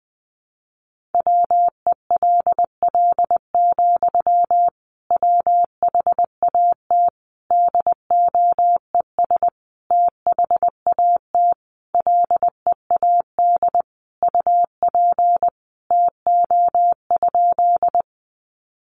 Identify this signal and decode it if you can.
Morse code, heard as a keyed tone, about 20 wpm, 700 Hz. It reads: WELL, WHAT DOES THAT LEAD UP TO?
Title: Sentences from Top 500 Words 20wpm